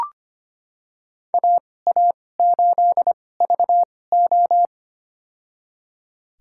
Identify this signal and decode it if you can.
Morse code, heard as a keyed tone, about 25 wpm, spaced as for 18 wpm, 700 Hz. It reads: AA8VO
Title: USA Call Signs - 5 Characters - 2x Character Spacing - 25wpm